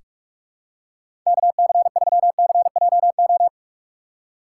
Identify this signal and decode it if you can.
Morse code, heard as a keyed tone, about 46 wpm, 700 Hz. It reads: KC3CJO
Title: USA Call Signs - 6 Characters - 6x Character Spacing - 45wpm